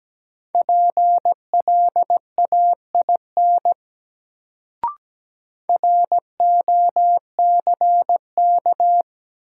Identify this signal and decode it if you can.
Morse code, heard as a keyed tone, about 17 wpm, 700 Hz. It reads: PLAIN ROCK
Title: Sentences from Top 400 Words - New Word Drill - 17wpm